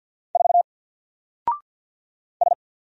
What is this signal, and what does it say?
Morse code, heard as a keyed tone, about 48 wpm, 700 Hz. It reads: 4 S
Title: Single Letter-Number 50wpm